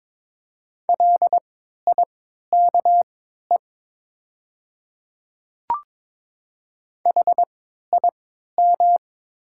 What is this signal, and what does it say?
Morse code, heard as a keyed tone, about 21 wpm, spaced as for 12 wpm, 700 Hz. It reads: LIKE HIM
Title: 3x Character Spacing - Top 100 Words - 22wpm